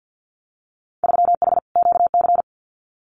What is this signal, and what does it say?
Morse code, heard as a keyed tone, about 50 wpm, 700 Hz. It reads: 3H QC
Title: Winter Field Day Exchanges 50wpm